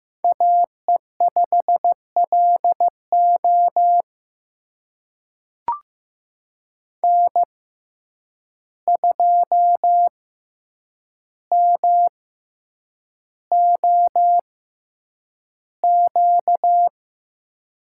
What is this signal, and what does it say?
Morse code, heard as a keyed tone, about 15 wpm, spaced as for 8 wpm, 700 Hz. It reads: AE5LO N2MOQ
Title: USA Call Signs - 5 Characters - 6x Character Spacing - 15wpm